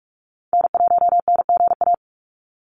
Morse code, heard as a keyed tone, about 46 wpm, 700 Hz. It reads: N1NGA
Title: USA Call Signs 45wpm